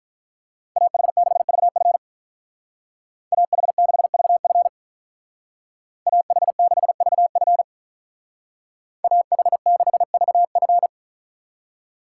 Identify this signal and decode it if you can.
Morse code, heard as a keyed tone, about 43 wpm, 700 Hz. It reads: AH6VF AH6VF AH6VF AH6VF
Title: Speed Racing - USA Call Signs 35wpm